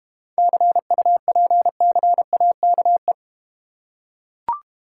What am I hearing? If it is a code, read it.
Morse code, heard as a keyed tone, about 32 wpm, 700 Hz. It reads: CUPCAKE